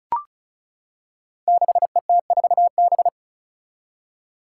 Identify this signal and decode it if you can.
Morse code, heard as a keyed tone, about 36 wpm, 700 Hz. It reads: 6ET4B
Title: Five Challenging Letter-Number Combinations 35wpm